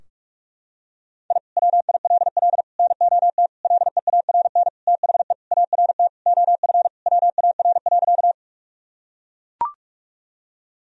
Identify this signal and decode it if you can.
Morse code, heard as a keyed tone, about 41 wpm, 700 Hz. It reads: I WILL NOT LEARN THE ART OF WAR.
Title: Sentences from Top 1000 Words - Review Part 3 - 45wpm